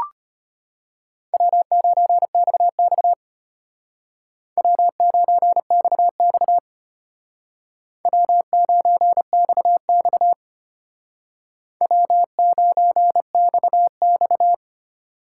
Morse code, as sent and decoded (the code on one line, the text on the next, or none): .-- ----. -..- -..- / .-- ----. -..- -..- / .-- ----. -..- -..- / .-- ----. -..- -..-
W9XX W9XX W9XX W9XX